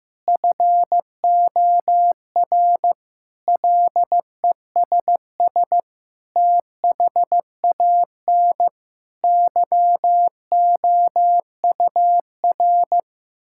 Morse code, heard as a keyed tone, about 15 wpm, 700 Hz. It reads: FOR LESS THAN YOUR
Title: Sentences from Top 1000 Words - Review Part 5 - 15wpm